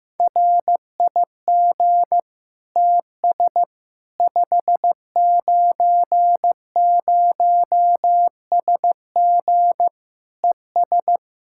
Morse code, heard as a keyed tone, about 15 wpm, 700 Hz. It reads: RIG TS 590SG ES
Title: K1USN SST QSO Practice 15wpm